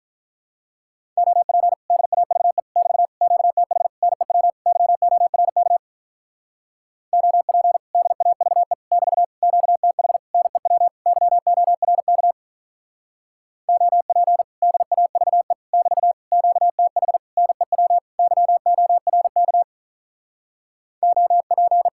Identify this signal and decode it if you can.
Morse code, heard as a keyed tone, about 43 wpm, 700 Hz. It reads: OP DAVE <BT> QTH NEW YORK OP DAVE <BT> QTH NEW YORK OP DAVE <BT> QTH NEW YORK OP
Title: Speed-Racing Common QSO Phrases - 35wpm